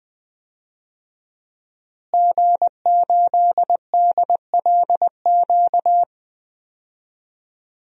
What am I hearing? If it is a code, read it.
Morse code, heard as a keyed tone, about 20 wpm, 700 Hz. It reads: G8DLQ
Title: DXCC-100 Call Signs - 5 Characters - Repeated 2x - 20wpm